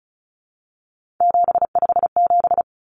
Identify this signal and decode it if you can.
Morse code, heard as a keyed tone, about 34 wpm, 700 Hz. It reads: 757
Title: Common Cut Numbers - 3-Digit 35wpm